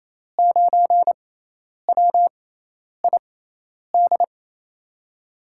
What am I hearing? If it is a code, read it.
Morse code, heard as a keyed tone, about 28 wpm, spaced as for 10 wpm, 700 Hz. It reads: 9WID
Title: USA Call Signs - 6 Characters - 6x Character Spacing - 28wpm